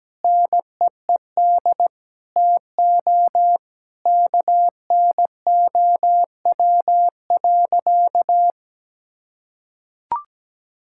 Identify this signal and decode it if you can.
Morse code, heard as a keyed tone, about 17 wpm, 700 Hz. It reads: NEED TO KNOW.
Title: Sentences from Top 1000 Words - Review Part 5 - 17wpm